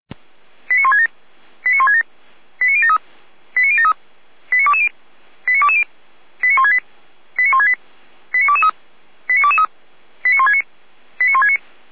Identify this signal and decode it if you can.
Five-tone selective call (ZVEI), heard as quick runs of five tones, tones of about 70 ms. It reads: 89167 89167 89073 89073 89220 89220 89168 89168 89223 89223 89169 89169